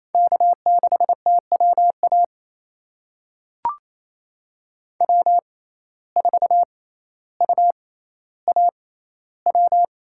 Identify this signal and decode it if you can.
Morse code, heard as a keyed tone, about 28 wpm, spaced as for 14 wpm, 700 Hz. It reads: K6TWA W4UAW
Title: USA Call Signs - 5 Characters - 6x Character Spacing - 28wpm